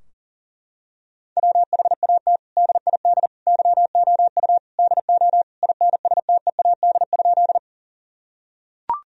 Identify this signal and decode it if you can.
Morse code, heard as a keyed tone, about 40 wpm, 700 Hz. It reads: WHAT DID YOU DO INSTEAD?
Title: Sentences from Top 1000 Words - Review Part 4 - 40wpm